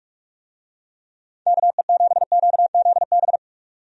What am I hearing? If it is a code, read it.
Morse code, heard as a keyed tone, about 46 wpm, 700 Hz. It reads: KE7QZB
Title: USA Call Signs - 6 Characters - 45wpm